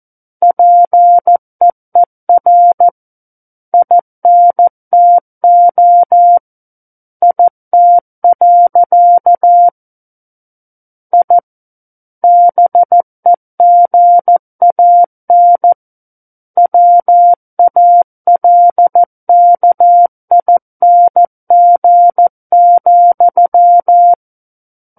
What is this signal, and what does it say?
Morse code, heard as a keyed tone, about 14 wpm, 700 Hz. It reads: PEER INTO IT. I BEGAN WALKING,
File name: Book One / 05-Chapter 5 - 14 WPM